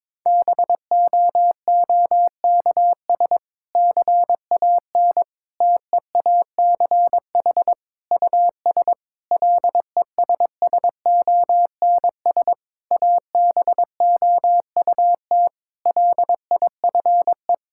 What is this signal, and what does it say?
Morse code, heard as a keyed tone, about 22 wpm, 700 Hz. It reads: BOOKS CAN TEACH US LESSONS ABOUT LIFE
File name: Difficult Sentences with Hints - 22wpm